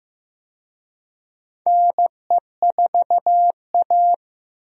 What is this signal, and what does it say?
Morse code, heard as a keyed tone, about 15 wpm, 700 Hz. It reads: NE4A
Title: USA Call Signs - 4 Characters - Repeated 3x - 15wpm